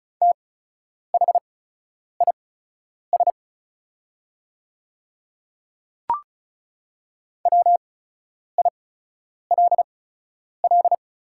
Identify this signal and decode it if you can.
Morse code, heard as a keyed tone, about 36 wpm, spaced as for 10 wpm, 700 Hz. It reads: THIS WILL
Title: 8x Character Spacing - Top 200 Words - 35wpm